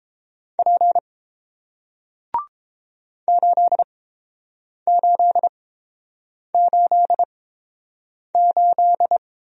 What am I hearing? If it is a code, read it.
Morse code, heard as a keyed tone, about 30 wpm, spaced as for 7 wpm, 700 Hz. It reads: P 8888